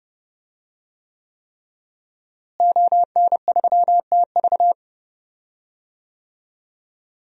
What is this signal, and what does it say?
Morse code, heard as a keyed tone, about 31 wpm, 700 Hz. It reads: ON3TV
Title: DXCC-100 Call Signs - 5 Characters - Repeated 3x - 30wpm